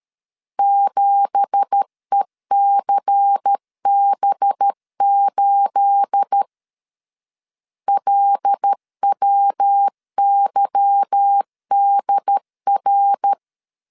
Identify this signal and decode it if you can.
Morse code, heard as a keyed tone, about 13 wpm, 800 Hz. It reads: 7ECB8 LWYDR
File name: CW / DET / 1 9 1500C S C 12wpm 3811